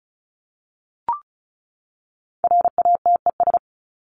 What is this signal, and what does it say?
Morse code, heard as a keyed tone, about 34 wpm, 700 Hz. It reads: RATES
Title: ES:S Suffix - Top 100 Words with Base Words 35wpm